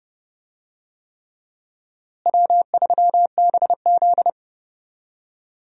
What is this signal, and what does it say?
Morse code, heard as a keyed tone, about 30 wpm, 700 Hz. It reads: W3BZ
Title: USA Call Signs - 4 Characters - Repeated 3x - 30wpm